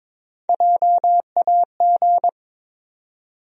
Morse code, heard as a keyed tone, about 22 wpm, 700 Hz. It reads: JAG
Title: USA Call Sign Postfixes - 3 Letters 22wpm